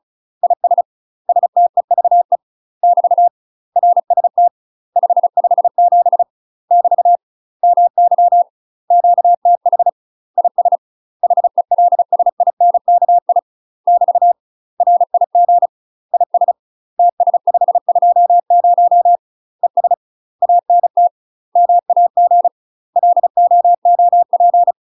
Morse code, heard as a keyed tone, about 36 wpm, 700 Hz. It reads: IS STEVE <BT> RST 557 <BT> MY QTH IS HELSINKI <BT> RIG IS TS520 ES ANT MAG LOOP
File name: UK MASTER QSOs TRAINING / 35WPM / UK MASTER QSOs 0029